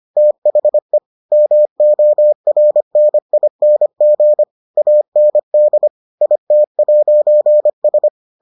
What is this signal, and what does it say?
Morse code, heard as a keyed tone, about 25 wpm, 600 Hz. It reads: THE MORNING AND IT'S